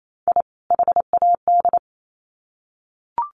Morse code, heard as a keyed tone, about 28 wpm, 700 Hz. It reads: I HAD
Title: Sets of 2 Words 28wpm